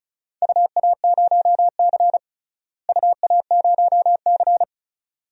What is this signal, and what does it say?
Morse code, heard as a keyed tone, about 34 wpm, 690 Hz. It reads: UA0C UA0C
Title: DXCC-100 Call Signs - 4 Characters - Repeated 2x - 35wpm